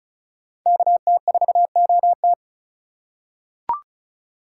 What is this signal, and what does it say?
Morse code, heard as a keyed tone, about 36 wpm, 700 Hz. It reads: KT4OT
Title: USA Call Signs - 5 Characters - 3x Character Spacing - 35wpm